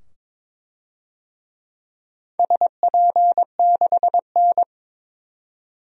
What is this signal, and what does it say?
Morse code, heard as a keyed tone, about 22 wpm, 700 Hz. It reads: SP6N
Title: DXCC-100 Call Signs - 4 Characters - 22wpm